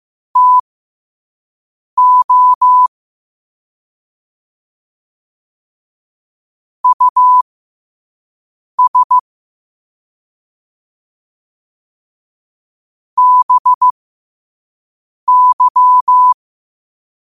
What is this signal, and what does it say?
Morse code, heard as a keyed tone, about 15 wpm, spaced as for 5 wpm, 1000 Hz. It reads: TO US BY
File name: qotdfiles / 5wpm